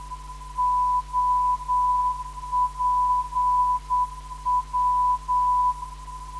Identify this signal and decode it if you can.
Morse code, heard as a keyed tone, about 9 wpm, 990 Hz. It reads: OPW
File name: ndb zvok / OPW - 267.5kHz